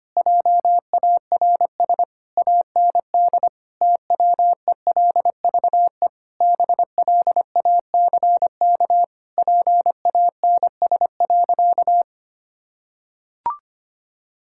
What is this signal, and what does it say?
Morse code, heard as a keyed tone, about 25 wpm, 700 Hz. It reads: JARS AND TWELVE BLACK PANS.
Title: Pangram Challenge v2 - 25wpm